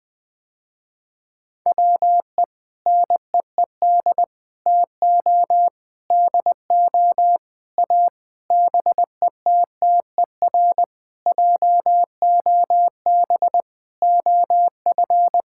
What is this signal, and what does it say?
Morse code, heard as a keyed tone, about 19 wpm, 700 Hz. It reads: WE NEED TO DO A BETTER JOB OF
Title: Sentences with 7-Word Phrases - Spoken Beginning - 20wpm